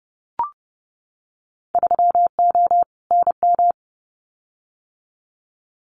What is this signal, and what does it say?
Morse code, heard as a keyed tone, about 30 wpm, 700 Hz. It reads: 3O NM
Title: Winter Field Day Exchanges Encore 30wpm